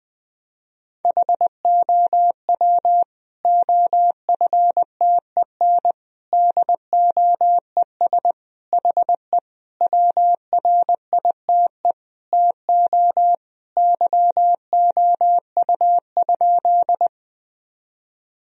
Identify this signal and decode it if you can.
Morse code, heard as a keyed tone, about 20 wpm, 700 Hz. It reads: HOW OFTEN DOES HE WRITE TO YOU?
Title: Sentences from Top 1000 Words - Review Part 1 - 20wpm